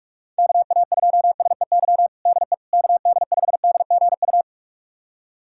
Morse code, heard as a keyed tone, about 41 wpm, 700 Hz. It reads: KA2SEY DE KD5DGU
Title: USA Callsign DE Callsign 45wpm